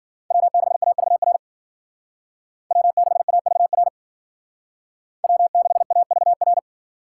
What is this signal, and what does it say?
Morse code, heard as a keyed tone, about 52 wpm, 700 Hz. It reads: W6AVR W6AVR W6AVR